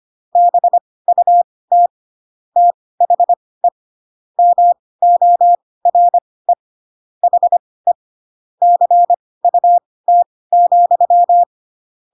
Morse code, heard as a keyed tone, about 25 wpm, spaced as for 18 wpm, 700 Hz. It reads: BUT THE MORE HE CUT,
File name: Pinocchio ch3 12 25 ik1hge